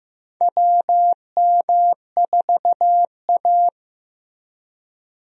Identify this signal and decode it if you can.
Morse code, heard as a keyed tone, about 15 wpm, 700 Hz. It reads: WM4A
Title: USA Call Signs - 4 Characters - Repeated 3x - 15wpm